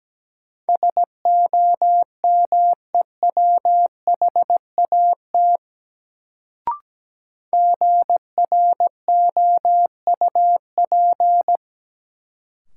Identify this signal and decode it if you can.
Morse code, heard as a keyed tone, about 17 wpm, 700 Hz. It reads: SOMEWHAT GROUP
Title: Sentences from Top 1000 Words - Review Part 1 - New Word Drill - 17wpm